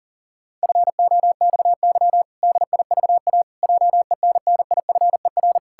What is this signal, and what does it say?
Morse code, heard as a keyed tone, about 40 wpm, 700 Hz. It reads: FOXY DIVA JENNIFER
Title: Pangram Challenge v2 - 40wpm